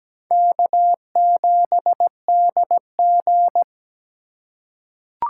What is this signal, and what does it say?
Morse code, heard as a keyed tone, about 17 wpm, 700 Hz. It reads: K7DG
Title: USA Call Signs - 3x Character Spacing - 17wpm